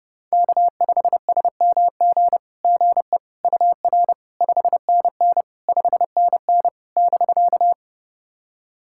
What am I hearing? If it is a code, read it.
Morse code, heard as a keyed tone, about 30 wpm, 700 Hz. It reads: K5SMG GE UR 5NN 5NN <BK>
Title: SOTA QSO Practice 30wpm